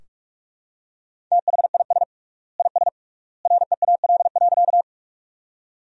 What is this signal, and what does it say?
Morse code, heard as a keyed tone, about 44 wpm, 700 Hz. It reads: THIS IS REAL.
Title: 3x Word-Spacing - Sentences from Top 300 Words - 45wpm